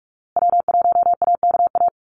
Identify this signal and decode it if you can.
Morse code, heard as a keyed tone, about 44 wpm, 700 Hz. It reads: W1AKA
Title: USA Call Signs Repeated 3x 45wpm